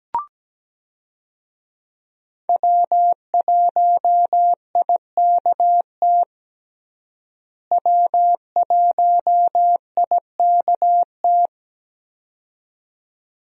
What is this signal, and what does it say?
Morse code, heard as a keyed tone, about 17 wpm, 700 Hz. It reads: W1IKT W1IKT
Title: USA Call Signs - 5 Characters - Repeated 2x - 17wpm